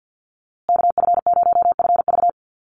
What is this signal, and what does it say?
Morse code, heard as a keyed tone, about 50 wpm, 700 Hz. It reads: KF0FV
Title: USA Call Signs Repeated 3x 50wpm